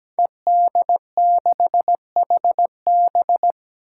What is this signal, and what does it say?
Morse code, heard as a keyed tone, about 17 wpm, 700 Hz. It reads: ED6HB
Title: Five Challenging Letter-Number Combinations 17wpm